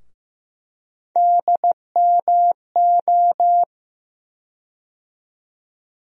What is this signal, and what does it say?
Morse code, heard as a keyed tone, about 15 wpm, 700 Hz. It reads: DMO
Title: USA Call Sign Postfixes - 3 Letters 15wpm